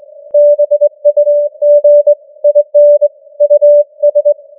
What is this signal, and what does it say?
Morse code, heard as a keyed tone, about 22 wpm, 590 Hz. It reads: BUG IN US
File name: challenge-178